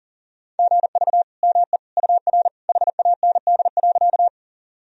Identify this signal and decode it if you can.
Morse code, heard as a keyed tone, about 37 wpm, 700 Hz. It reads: GV ME UR HAND.
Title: Sentences from Top 200 Words and Abbreviations 40wpm